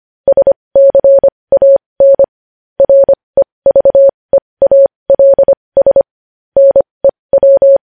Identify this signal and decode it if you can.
Morse code, heard as a keyed tone, about 25 wpm, spaced as for 20 wpm, 550 Hz. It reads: SCAN REVEALS NEW
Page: news headlines in morse code at 20 wpm